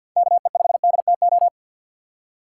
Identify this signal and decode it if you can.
Morse code, heard as a keyed tone, about 61 wpm, 700 Hz. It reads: KE5DTO
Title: Speed Racing - USA 2x3 Call Signs 50wpm